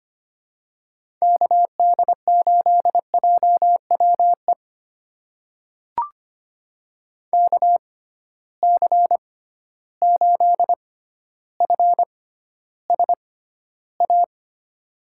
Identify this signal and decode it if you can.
Morse code, heard as a keyed tone, about 25 wpm, spaced as for 13 wpm, 700 Hz. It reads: KD8JWE KC8FSA
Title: USA Call Signs - 6 Characters - 6x Character Spacing - 25wpm